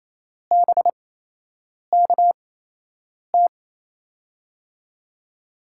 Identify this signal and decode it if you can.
Morse code, heard as a keyed tone, about 27 wpm, spaced as for 8 wpm, 700 Hz. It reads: BKT
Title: USA Call Signs - 8x Character Spacing - 28wpm